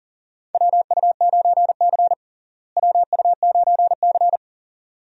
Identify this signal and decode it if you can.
Morse code, heard as a keyed tone, about 40 wpm, 700 Hz. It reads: WU9C WU9C
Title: USA Call Signs - 4 Characters - Repeated 2x - 40wpm